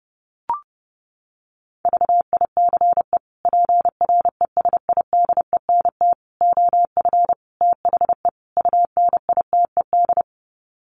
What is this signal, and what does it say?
Morse code, heard as a keyed tone, about 30 wpm, 700 Hz. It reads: VICE PRESIDENT OF THE UNITED
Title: Sets of 5 Words Encore2 30wpm